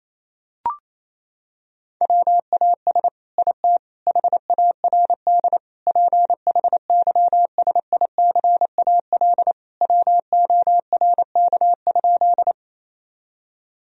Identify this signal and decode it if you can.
Morse code, heard as a keyed tone, about 28 wpm, 700 Hz. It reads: WAS IT HARD PHYSICAL WORK?
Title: Sentences from Top 1000 Words - Review Part 4 - 28wpm